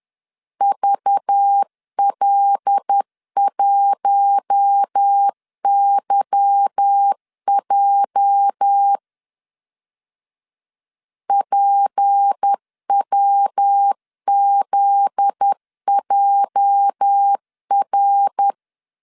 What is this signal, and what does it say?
Morse code, heard as a keyed tone, about 10 wpm, 800 Hz. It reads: VL1YJ PWZJR
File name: LEZIONI / 8 YJ45 / 8 YJ 10wpm 2651